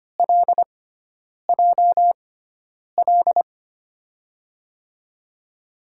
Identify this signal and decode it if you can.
Morse code, heard as a keyed tone, about 25 wpm, spaced as for 9 wpm, 700 Hz. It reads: LJL
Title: USA Call Signs - 6 Characters - 6x Character Spacing - 25wpm